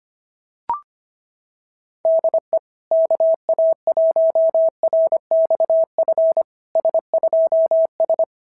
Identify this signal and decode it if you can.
Morse code, heard as a keyed tone, about 25 wpm, 650 Hz. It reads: DE KA1RXF S2S
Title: SOTA QSO Practice 25wpm